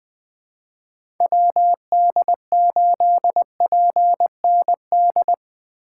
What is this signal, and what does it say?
Morse code, heard as a keyed tone, about 20 wpm, 700 Hz. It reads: WD8PND
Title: USA Call Signs - 6 Characters - 4x Character Spacing - 20wpm